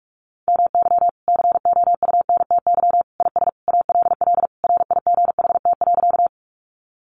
Morse code, heard as a keyed tone, about 44 wpm, 700 Hz. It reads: MY COUNTY IS ALL RIGHT.
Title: Sentences from Top 1200 Words 45wpm